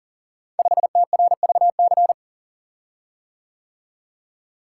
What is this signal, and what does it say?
Morse code, heard as a keyed tone, about 40 wpm, 700 Hz. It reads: 5TRVC
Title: Code Groups - 5 Letter:Numbers - 40wpm